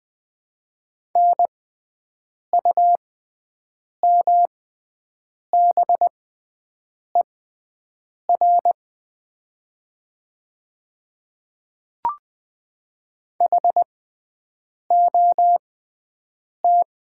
Morse code, heard as a keyed tone, about 20 wpm, spaced as for 7 wpm, 700 Hz. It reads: NUMBER HOT
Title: 6x Character Spacing - Top 100 Words - 20wpm